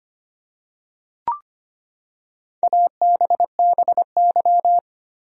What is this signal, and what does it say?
Morse code, heard as a keyed tone, about 25 wpm, 700 Hz. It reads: ABBY
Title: USA Common Names - Max 5 Letters - 25wpm